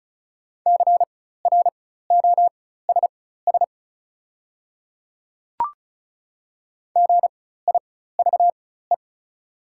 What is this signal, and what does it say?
Morse code, heard as a keyed tone, about 36 wpm, spaced as for 16 wpm, 690 Hz. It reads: CROSS GIVE
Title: 4x Character Spacing - Top 300 Words - 35wpm